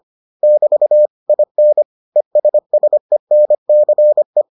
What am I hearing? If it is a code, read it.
Morse code, heard as a keyed tone, about 25 wpm, 600 Hz. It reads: <BT> IN ESSENCE